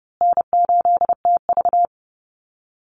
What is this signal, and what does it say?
Morse code, heard as a keyed tone, about 30 wpm, 700 Hz. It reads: N8TV
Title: USA Call Signs Repeated 3x 30wpm